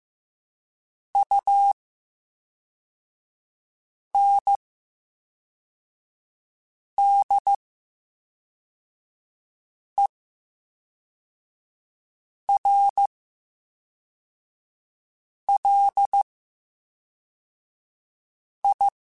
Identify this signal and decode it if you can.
Morse code, heard as a keyed tone, about 15 wpm, spaced as for 3 wpm, 800 Hz. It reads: UNDERLI
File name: Special INGs / Special INGs-mp3s / VerySlowToFaster Ing @10